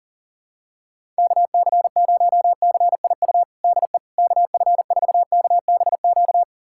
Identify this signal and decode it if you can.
Morse code, heard as a keyed tone, about 40 wpm, 700 Hz. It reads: KC0CIU DE KF4KBQ
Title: USA Callsign DE Callsign - 6 Characters - 40wpm